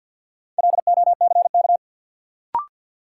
Morse code, heard as a keyed tone, about 52 wpm, 700 Hz. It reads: LOCK
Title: Sentences from Top 1300 Words - New Word Drill Alt - 50wpm